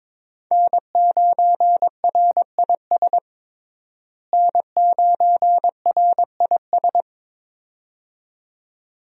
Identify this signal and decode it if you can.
Morse code, heard as a keyed tone, about 22 wpm, 700 Hz. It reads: N9RIS N9RIS
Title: USA Call Signs - 5 Characters - Repeated 2x - 22wpm